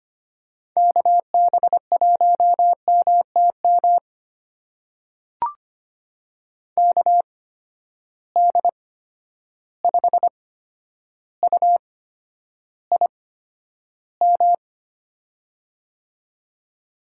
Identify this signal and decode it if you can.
Morse code, heard as a keyed tone, about 25 wpm, spaced as for 11 wpm, 700 Hz. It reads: KB1MTM KD5UIM
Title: USA Call Signs - 6 Characters - 8x Character Spacing - 25wpm